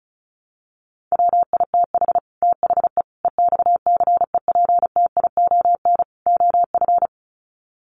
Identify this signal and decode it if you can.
Morse code, heard as a keyed tone, about 34 wpm, 700 Hz. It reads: WITH THE EXCEPTION OF
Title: Sets of 4 Words 35wpm